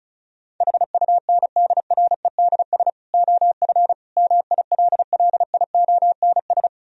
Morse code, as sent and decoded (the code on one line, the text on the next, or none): .... ..- -. -.. .-. -.. ... / --- ..-. / -- .. .-.. .-.. .. --- -. ...
HUNDREDS OF MILLIONS